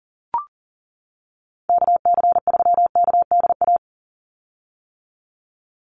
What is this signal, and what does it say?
Morse code, heard as a keyed tone, about 40 wpm, 700 Hz. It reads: KC3KDA
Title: USA Call Signs 40wpm